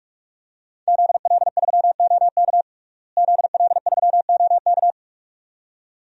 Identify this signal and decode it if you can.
Morse code, heard as a keyed tone, about 46 wpm, 700 Hz. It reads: ZL3OK ZL3OK
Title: DXCC-100 Call Signs - 5 Characters - Repeated 2x - 45wpm